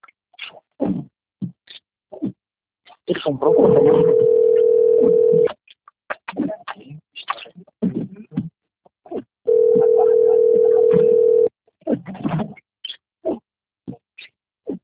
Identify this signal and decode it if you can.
Ringback tone (North America), 440 + 480 Hz, with speech.
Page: Recife - PE